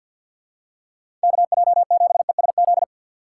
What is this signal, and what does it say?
Morse code, heard as a keyed tone, about 47 wpm, 700 Hz. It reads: KJ7ESZ
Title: USA Call Signs - 6 Characters - 3x Character Spacing - 50wpm